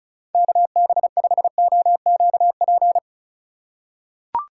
Morse code, heard as a keyed tone, about 34 wpm, 700 Hz. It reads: KB5OQP
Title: USA Call Signs - 6 Characters - 2x Character Spacing - 35wpm